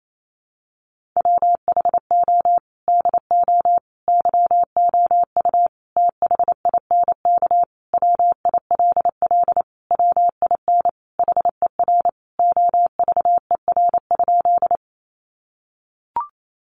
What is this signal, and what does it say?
Morse code, heard as a keyed tone, about 28 wpm, 700 Hz. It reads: WHO DO YOU THINK WILL WIN HER OVER?
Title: Sentences from Top 1500 Words 28wpm